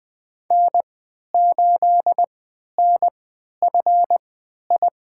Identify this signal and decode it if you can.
Morse code, heard as a keyed tone, about 20 wpm, spaced as for 11 wpm, 700 Hz. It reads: N8NFI
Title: USA Call Signs - 5 Characters - 3x Character Spacing - 20wpm